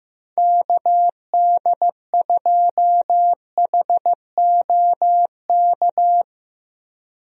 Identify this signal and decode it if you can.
Morse code, heard as a keyed tone, about 15 wpm, 700 Hz. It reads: KD2HOK
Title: USA Call Signs - 6 Characters - Repeated 2x - 15wpm